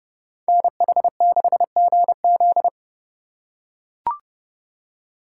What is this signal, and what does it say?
Morse code, heard as a keyed tone, about 30 wpm, 700 Hz. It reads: NH6GZ